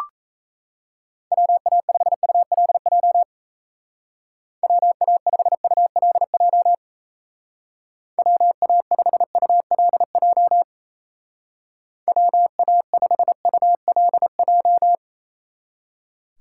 Morse code, heard as a keyed tone, about 36 wpm, 700 Hz. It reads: WA5ULJ WA5ULJ WA5ULJ WA5ULJ